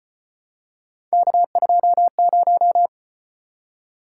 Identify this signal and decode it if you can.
Morse code, heard as a keyed tone, about 32 wpm, 700 Hz. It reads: K20